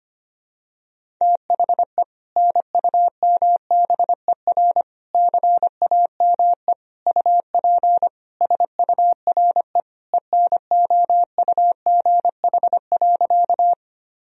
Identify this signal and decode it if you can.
Morse code, heard as a keyed tone, about 25 wpm, 700 Hz. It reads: THE NUMBER CAME UP SURE ENOUGH.
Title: Sentences from Top 1000 Words - Review Part 1 - 25wpm